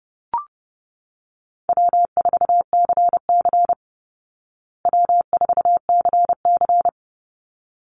Morse code, heard as a keyed tone, about 30 wpm, 700 Hz. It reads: W4CC W4CC
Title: Easy Call Signs Repeated 2x - 30wpm